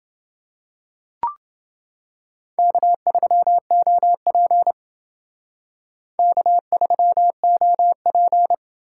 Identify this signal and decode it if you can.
Morse code, heard as a keyed tone, about 29 wpm, 700 Hz. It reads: K3OP K3OP